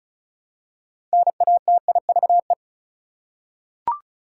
Morse code, heard as a keyed tone, about 36 wpm, 700 Hz. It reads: NATIVE
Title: Six Letter Words - 35wpm